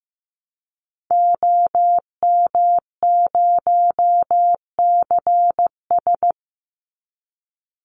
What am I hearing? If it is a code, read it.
Morse code, heard as a keyed tone, about 15 wpm, 700 Hz. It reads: OM0CS
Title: Call Signs Repeated 3x - 15wpm